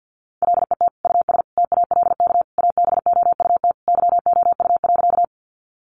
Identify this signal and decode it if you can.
Morse code, heard as a keyed tone, about 50 wpm, 700 Hz. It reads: LET US TALK ABOUT YOU.
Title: Sentences from Top 400 Words 50wpm